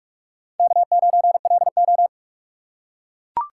Morse code, heard as a keyed tone, about 41 wpm, 700 Hz. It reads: K9LO